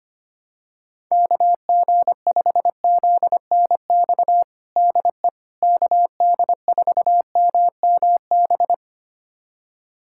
Morse code, heard as a keyed tone, about 25 wpm, 700 Hz. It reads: KG5ZNX DE KD4MMB
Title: USA Callsign DE Callsign 25wpm